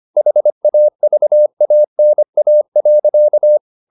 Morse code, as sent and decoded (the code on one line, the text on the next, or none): .... .- ...- .- -. .- .-.-.-
HAVANA.